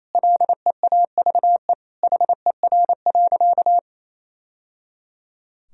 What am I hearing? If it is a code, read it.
Morse code, heard as a keyed tone, about 28 wpm, 700 Hz. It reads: LEAVE HER.
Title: Sentences from Top 1000 Words - Review Part 1 - 28wpm